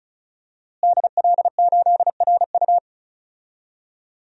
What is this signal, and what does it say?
Morse code, heard as a keyed tone, about 36 wpm, 700 Hz. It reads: DL8RU
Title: DXCC-100 Call Signs - 5 Characters - Repeated 2x - 35wpm